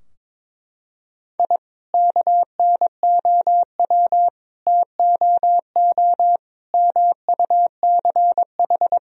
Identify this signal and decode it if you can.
Morse code, heard as a keyed tone, about 21 wpm, 700 Hz. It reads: I KNOW TOO MUCH
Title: Sentences from Top 1000 Words - Review Part 3 - 22wpm